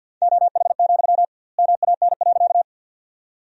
Morse code, heard as a keyed tone, about 49 wpm, 700 Hz. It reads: OH, MAN.
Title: 2x Word-Spacing - Sentences from Top 500 Words - 50wpm